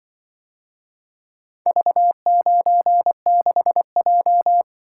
Morse code, heard as a keyed tone, about 25 wpm, 700 Hz. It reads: V96J
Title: Speed-Racing - Four Letter-Number Combinations - 20wpm